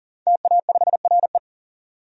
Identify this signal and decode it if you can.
Morse code, heard as a keyed tone, about 40 wpm, 700 Hz. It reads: TA5RE
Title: DXCC-100 Call Signs - 5 Characters - 40wpm